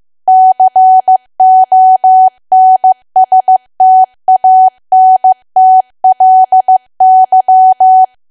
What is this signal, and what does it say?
Morse code, heard as a keyed tone, about 15 wpm, 750 Hz. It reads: CONSTANTLY
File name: cwa / W1AW-Level2 / 15WPM